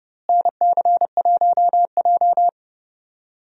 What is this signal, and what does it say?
Morse code, heard as a keyed tone, about 29 wpm, 700 Hz. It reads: NC1J
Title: USA Call Signs - 4 Characters - Repeated 3x - 30wpm